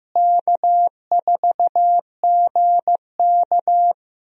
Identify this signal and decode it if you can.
Morse code, heard as a keyed tone, about 14 wpm, 700 Hz. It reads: K4GK
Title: USA Call Signs - 4 Characters - Repeated 3x - 15wpm